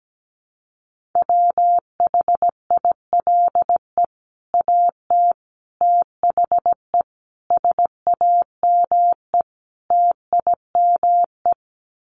Morse code, heard as a keyed tone, about 17 wpm, 700 Hz. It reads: WHILE AT THE SAME TIME
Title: Sets of 5 Words 17wpm